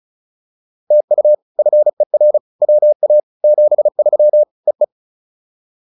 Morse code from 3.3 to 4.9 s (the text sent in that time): 73 EE